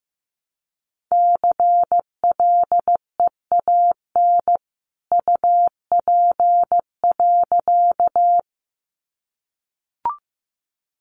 Morse code, heard as a keyed tone, about 15 wpm, 700 Hz. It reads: CLEAN UP.
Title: A Sound Of Thunder 15wpm